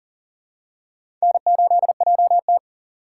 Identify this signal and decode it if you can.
Morse code, heard as a keyed tone, about 42 wpm, 690 Hz. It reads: N8JT
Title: USA Call Signs - 4 Characters - 6x Character Spacing - 40wpm